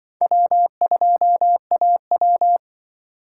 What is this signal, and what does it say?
Morse code, heard as a keyed tone, about 25 wpm, 700 Hz. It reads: W2AW
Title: Speed Racing - USA 1x2 Call Signs 20wpm